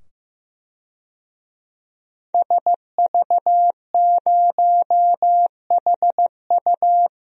Morse code, heard as a keyed tone, about 15 wpm, 700 Hz. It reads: SV0HU
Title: DXCC-100 Call Signs - 5 Characters - 15wpm